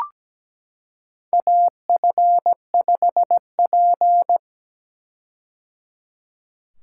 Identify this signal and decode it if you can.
Morse code, heard as a keyed tone, about 17 wpm, 700 Hz. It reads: AF5P
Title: K1USN SST QSO Practice - 17wpm